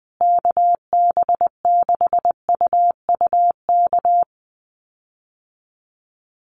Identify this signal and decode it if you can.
Morse code, heard as a keyed tone, about 20 wpm, 700 Hz. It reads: KB6UUK
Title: USA Call Signs Repeated 3x 20wpm